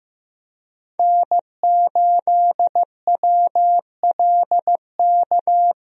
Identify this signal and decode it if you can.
Morse code, heard as a keyed tone, about 15 wpm, 700 Hz. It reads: N8WLK
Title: USA Call Signs - 5 Characters - Repeated 3x - 15wpm